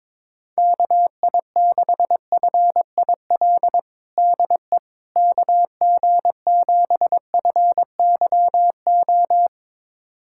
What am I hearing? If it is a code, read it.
Morse code, heard as a keyed tone, about 22 wpm, 700 Hz. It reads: KI6FIL DE KG7FYO
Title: USA Callsign DE Callsign 22wpm